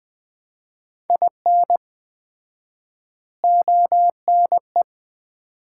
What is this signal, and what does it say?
Morse code, heard as a keyed tone, about 20 wpm, 700 Hz. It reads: IN ONE
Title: 4x Word-Spacing - Sets of 3 Words - 20wpm